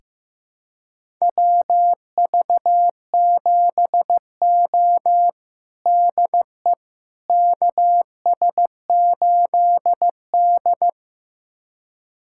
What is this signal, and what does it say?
Morse code, heard as a keyed tone, about 15 wpm, 700 Hz. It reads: WV7O DE KS8D
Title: USA Callsign DE Callsign - 4 Characters - 15wpm